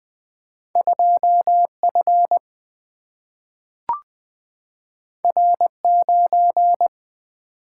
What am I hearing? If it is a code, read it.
Morse code, heard as a keyed tone, about 20 wpm, 700 Hz. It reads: 2F R9
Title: Single-Letter Single-Number 20wpm